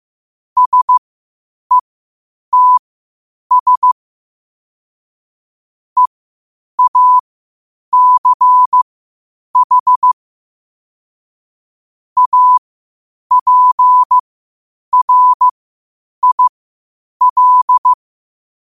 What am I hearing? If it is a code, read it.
Morse code, heard as a keyed tone, about 15 wpm, spaced as for 8 wpm, 1000 Hz. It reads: SETS EACH APRIL
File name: qotdfiles / 8wpm